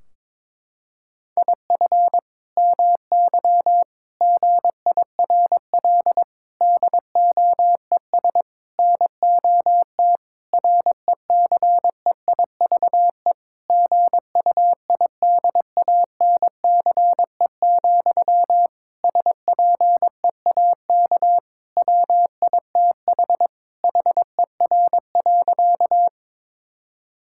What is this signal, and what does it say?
Morse code, heard as a keyed tone, about 22 wpm, 700 Hz. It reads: IF MY GIRL DOES NOT RECEIVE GUIDANCE, SPEAK WITH HER.
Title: Sentences from Top 1600 Words 22wpm